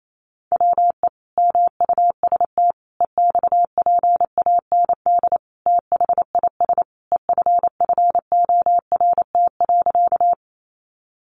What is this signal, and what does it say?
Morse code, heard as a keyed tone, about 28 wpm, 700 Hz. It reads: WE MUST EXPAND THIS EFFORT.
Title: Sentences from Top 1400 Words 28wpm